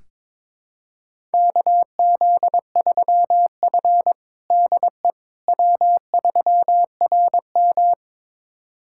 Morse code, heard as a keyed tone, about 22 wpm, 700 Hz. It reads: KZ3F DE W3RM
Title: USA Callsign DE Callsign - 4 Characters - 22wpm